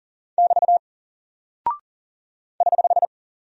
Morse code, heard as a keyed tone, about 39 wpm, 700 Hz. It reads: <BT> <HH>
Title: Single Character (Letter:Number:Punctuation:Prosign) 40wpm